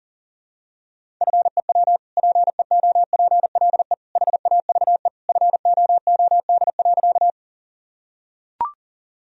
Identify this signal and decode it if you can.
Morse code, heard as a keyed tone, about 40 wpm, 700 Hz. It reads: FEW PEOPLE HAVE FOOD.
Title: Sentences from Top 1000 Words - Review Part 1 - 40wpm